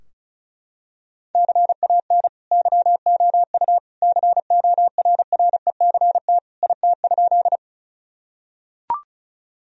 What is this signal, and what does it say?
Morse code, heard as a keyed tone, about 34 wpm, 700 Hz. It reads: CAN YOU CORRECT IT?
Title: Sentences from Top 1000 Words - Review Part 2 - 35wpm